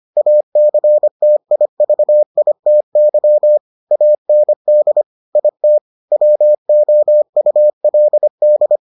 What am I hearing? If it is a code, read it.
Morse code, heard as a keyed tone, about 25 wpm, 600 Hz. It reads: ACTIVITY AND IT WOULD